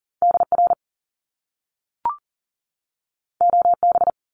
Morse code, heard as a keyed tone, about 40 wpm, 700 Hz. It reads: DR OB
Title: Top Two Letter Combinations in Words 40wpm